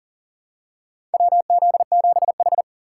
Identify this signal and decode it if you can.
Morse code, heard as a keyed tone, about 40 wpm, 700 Hz. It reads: WZ7H
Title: USA Call Signs - 4 Characters - 2x Character Spacing - 40wpm